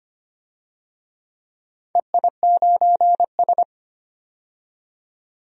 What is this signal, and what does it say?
Morse code, heard as a keyed tone, about 25 wpm, 700 Hz. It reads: EI9S
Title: DXCC-100 Call Signs - 4 Characters - 25wpm